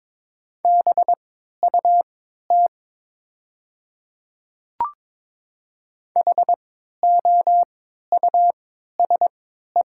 Morse code, heard as a keyed tone, about 22 wpm, spaced as for 13 wpm, 700 Hz. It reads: BUT HOUSE
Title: 3x Character Spacing - Top 200 Words - 22wpm